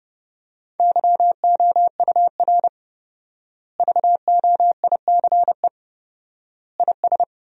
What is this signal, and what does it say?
Morse code, heard as a keyed tone, about 32 wpm, 700 Hz. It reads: YOUR VOICE IS
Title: 4x Word-Spacing - Sentences from Top 500 Words - 30wpm